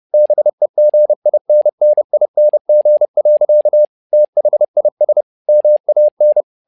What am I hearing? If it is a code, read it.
Morse code, heard as a keyed tone, about 30 wpm, 600 Hz. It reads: BEGINNING. THIS MAN
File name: learning center / tower trouble / tower trouble 30wpm